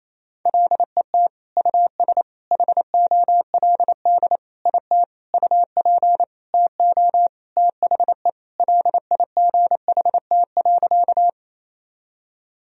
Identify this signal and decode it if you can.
Morse code, heard as a keyed tone, about 28 wpm, 700 Hz. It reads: LET US HOLD IT UP TO THE LIGHT.
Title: Sentences from Top 1000 Words - Review Part 2 - 28wpm